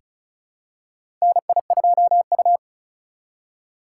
Morse code, heard as a keyed tone, about 36 wpm, 700 Hz. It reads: NI2U
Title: USA Call Signs - 4 Characters - 4x Character Spacing - 35wpm